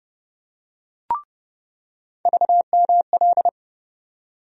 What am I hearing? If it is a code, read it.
Morse code, heard as a keyed tone, about 30 wpm, 700 Hz. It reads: VML
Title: USA Call Sign Postfixes - 3 Letters 30wpm